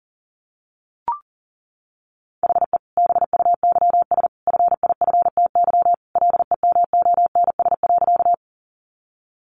Morse code, heard as a keyed tone, about 38 wpm, 700 Hz. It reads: HE BUYS FIFTY LEMONS.
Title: Additional Easy Phrases 40wpm